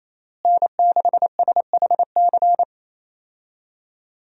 Morse code, heard as a keyed tone, about 28 wpm, 700 Hz. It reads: N6SHC
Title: USA Call Signs - 5 Characters - Repeated 3x - 28wpm